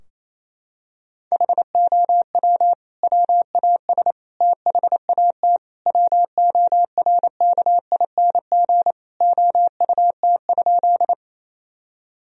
Morse code, heard as a keyed tone, about 28 wpm, 700 Hz. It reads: HOW WAS THAT WORKING OUT?
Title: Sentences from Top 1000 Words - Review Part 4 - 28wpm